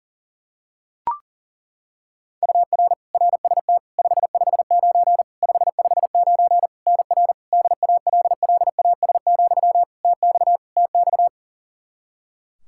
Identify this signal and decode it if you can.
Morse code, heard as a keyed tone, about 40 wpm, 700 Hz. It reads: UR RST 559 559 NR DALLAS, TX TX
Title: Practice QSO - Exchange 1 - Beginning - 40wpm